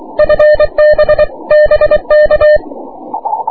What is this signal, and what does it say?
Morse code, heard as a keyed tone, about 23 wpm, 600 Hz. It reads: FB BK
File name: wp / 6m aurora cw 22jul 2009